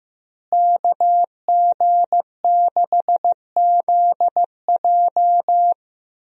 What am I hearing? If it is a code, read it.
Morse code, heard as a keyed tone, about 15 wpm, 700 Hz. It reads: KG6ZJ
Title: USA Call Signs - 5 Characters - Repeated 3x - 15wpm